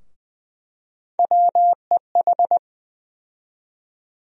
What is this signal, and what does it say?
Morse code, heard as a keyed tone, about 21 wpm, 700 Hz. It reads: WEH
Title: USA Call Sign Postfixes - 3 Letters 20wpm